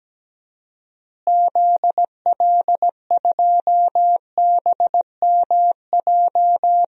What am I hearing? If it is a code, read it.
Morse code, heard as a keyed tone, about 17 wpm, 700 Hz. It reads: ZL2BMJ
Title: DXCC-100 Call Signs - 6 Characters - Repeated 3x - 17wpm